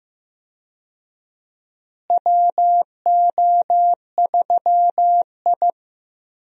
Morse code, heard as a keyed tone, about 15 wpm, 700 Hz. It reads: WO3I
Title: USA Call Signs - 4 Characters - Repeated 3x - 15wpm